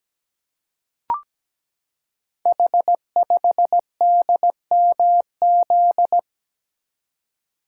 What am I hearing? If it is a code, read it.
Morse code, heard as a keyed tone, about 17 wpm, 690 Hz. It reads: H5DMZ
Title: Code Groups - 5 Letter:Numbers - 17wpm